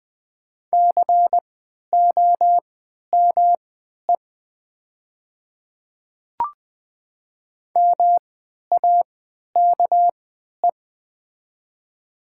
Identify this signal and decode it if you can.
Morse code, heard as a keyed tone, about 20 wpm, spaced as for 11 wpm, 700 Hz. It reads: COME MAKE
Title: 3x Character Spacing - Top 100 Words - 20wpm